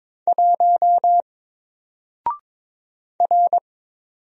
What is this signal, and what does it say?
Morse code, heard as a keyed tone, about 22 wpm, 700 Hz. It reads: 1 R
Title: Single Letter-Number Alt 22wpm